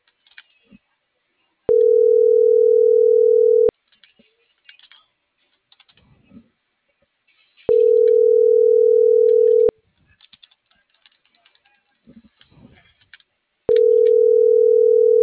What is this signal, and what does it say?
Ringback tone (North America), 440 + 480 Hz.